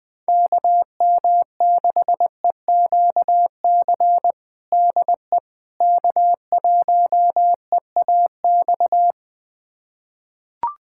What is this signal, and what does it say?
Morse code, heard as a keyed tone, about 20 wpm, 700 Hz. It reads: KM6EQC DE K1EAX
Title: USA Callsign DE Callsign 20wpm